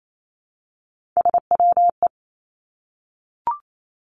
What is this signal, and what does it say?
Morse code, heard as a keyed tone, about 28 wpm, 700 Hz. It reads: SWE
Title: Three Challenging Letter-Number Combinations 28wpm